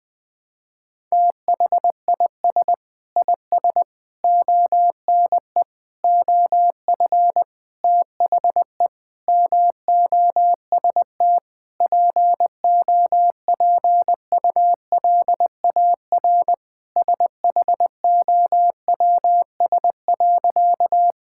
Morse code, heard as a keyed tone, about 20 wpm, 700 Hz. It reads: THIS IS ONE OF THE MOST POPULAR SHOWS.
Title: Sentences with 5-Word Phrases - 20wpm